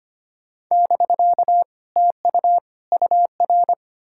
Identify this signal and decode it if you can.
Morse code, heard as a keyed tone, about 33 wpm, 700 Hz. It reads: <BK> TU UR